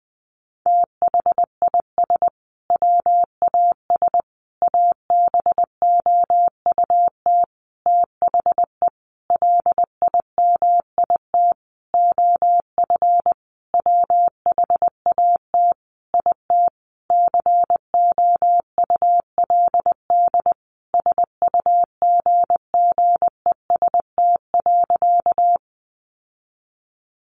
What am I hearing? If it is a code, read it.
Morse code, heard as a keyed tone, about 20 wpm, 700 Hz. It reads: THIS WAS ABOUT THE LIMIT OF WHAT IT COULD SUGGEST.
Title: Sentences from Top 1500 Words 20wpm